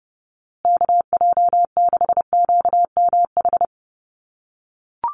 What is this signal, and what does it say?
Morse code, heard as a keyed tone, about 30 wpm, 700 Hz. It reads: KJ6QMH
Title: USA Call Signs Repeated 2x 30wpm